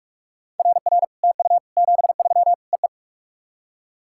Morse code, heard as a keyed tone, about 44 wpm, 690 Hz. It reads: RR TU 73 EE
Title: SOTA QSO Practice 45wpm